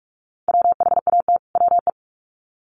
Morse code, heard as a keyed tone, about 44 wpm, 700 Hz. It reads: WHAT WE
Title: Sets of 2 Words 45wpm